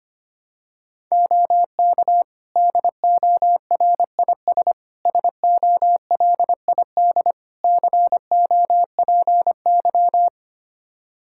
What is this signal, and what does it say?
Morse code, heard as a keyed tone, about 25 wpm, 700 Hz. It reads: OK DORIS SOLID COPY
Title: Practice QSO - Exchange 3 - More Info - 25wpm